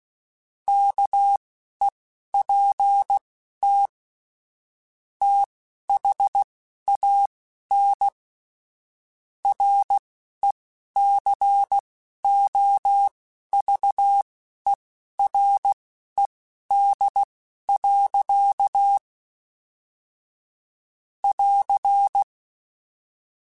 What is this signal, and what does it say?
Morse code, heard as a keyed tone, about 16 wpm, spaced as for 11 wpm, 800 Hz. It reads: KEPT THAN RECOVERED. <AR>
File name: Quotes / Quotes-mp3s / Character @10